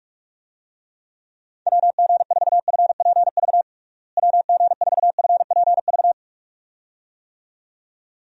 Morse code, heard as a keyed tone, about 46 wpm, 700 Hz. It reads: WG4FPV WG4FPV
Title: USA Call Signs - 6 Characters - Repeated 2x - 45wpm